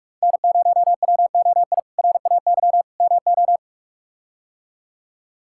Morse code, heard as a keyed tone, about 43 wpm, 700 Hz. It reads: N0WOI RAY MO
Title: K1USN SST QSO Practice - 45wpm